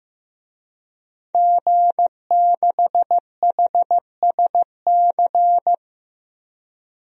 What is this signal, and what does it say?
Morse code, heard as a keyed tone, about 15 wpm, 700 Hz. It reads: G6HSC
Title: DXCC-100 Call Signs - 5 Characters - Repeated 2x - 15wpm